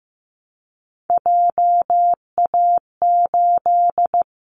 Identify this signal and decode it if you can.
Morse code, heard as a keyed tone, about 15 wpm, 700 Hz. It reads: JA8
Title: Callsign Prefixes for Top 50 DXCC - 15wpm